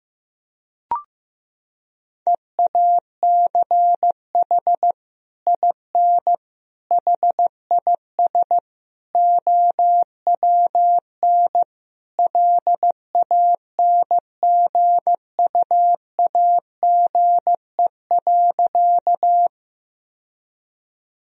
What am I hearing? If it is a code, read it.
Morse code, heard as a keyed tone, about 15 wpm, 700 Hz. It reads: EACH IN HIS OWN LANGUAGE.
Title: Sentences from Top 1000 Words - Review Part 3 - 15wpm